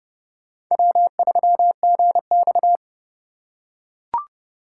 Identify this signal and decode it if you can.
Morse code, heard as a keyed tone, about 31 wpm, 690 Hz. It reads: W3GX